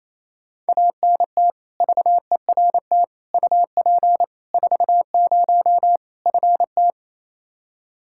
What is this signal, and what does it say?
Morse code, heard as a keyed tone, about 28 wpm, 700 Hz. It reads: ANT VERT UP 40 FT
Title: Practice QSO - Exchange 2 - Info - 28wpm